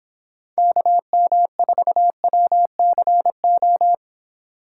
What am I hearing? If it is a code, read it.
Morse code, heard as a keyed tone, about 29 wpm, 700 Hz. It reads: KM4WCO